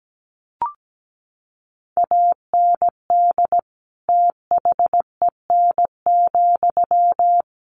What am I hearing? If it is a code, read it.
Morse code, heard as a keyed tone, about 17 wpm, 700 Hz. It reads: AND THEN,
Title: Sentences from Top 1500 Words 17wpm